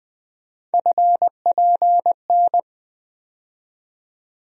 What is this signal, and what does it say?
Morse code, heard as a keyed tone, about 20 wpm, 700 Hz. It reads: FPN
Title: USA Call Sign Postfixes - 3 Letters 20wpm